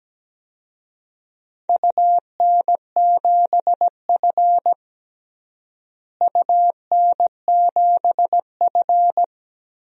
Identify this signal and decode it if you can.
Morse code, heard as a keyed tone, about 17 wpm, 700 Hz. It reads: UN7F UN7F
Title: DXCC-100 Call Signs - 4 Characters - Repeated 2x - 17wpm